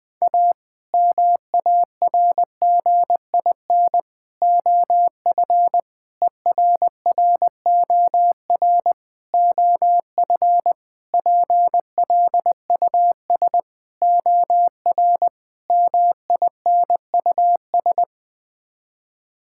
Morse code, heard as a keyed tone, about 20 wpm, 700 Hz. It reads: A MARGIN OF ERROR OF PLUS OR MINUS
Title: Sets of 8 Words - 20wpm